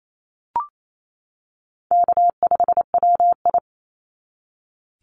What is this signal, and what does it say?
Morse code, heard as a keyed tone, about 28 wpm, 700 Hz. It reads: K5WI
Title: NA SOTA Chaser Callsigns 28wpm